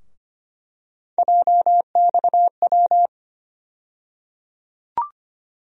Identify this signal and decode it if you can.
Morse code, heard as a keyed tone, about 25 wpm, 700 Hz. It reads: JXW
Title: USA Call Sign Postfixes - 3 Letters 25wpm